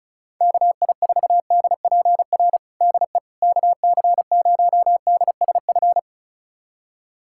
Morse code, heard as a keyed tone, about 36 wpm, 700 Hz. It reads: KI4DPR DE KC0DSF
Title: USA Callsign DE Callsign 35wpm